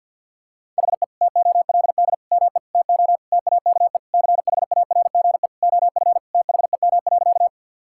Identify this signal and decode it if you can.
Morse code, heard as a keyed tone, about 50 wpm, 700 Hz. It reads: HE TOLD ME TO TAKE CHARGE OF THEM.
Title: Sentences from Top 1000 Words - Review Part 6 - 50wpm